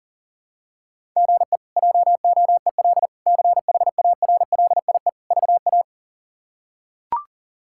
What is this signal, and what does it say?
Morse code, heard as a keyed tone, about 40 wpm, 700 Hz. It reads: GE JOEL CHARLIE VA
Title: K1USN SST QSO Practice - 40wpm